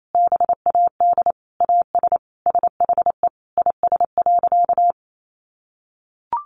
Morse code, heard as a keyed tone, about 28 wpm, 700 Hz. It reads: BAD AS SHE IS.
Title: Sentences from Top 1300 Words 28wpm